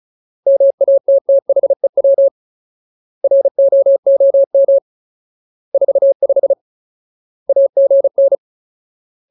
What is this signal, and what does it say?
Morse code, heard as a keyed tone, about 34 wpm, 550 Hz. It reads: MATTHEW ROOM 45 AGN
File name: Ultimate Word List / 35WPM Ultimate